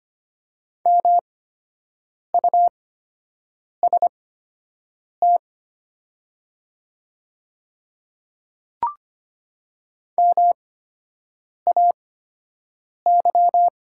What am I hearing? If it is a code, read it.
Morse code, heard as a keyed tone, about 25 wpm, spaced as for 7 wpm, 700 Hz. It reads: MUST MAY
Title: 8x Character Spacing - Top 300 Words - 25wpm